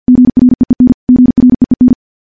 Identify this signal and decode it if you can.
Morse code, heard as a keyed tone, about 50 wpm, 250 Hz. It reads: GREEN GREEN
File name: morse / morse-training-50WPM